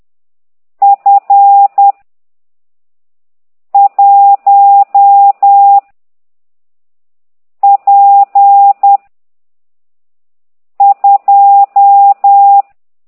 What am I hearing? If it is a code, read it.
Morse code, heard as a keyed tone, about 10 wpm, 800 Hz. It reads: F 1 P 2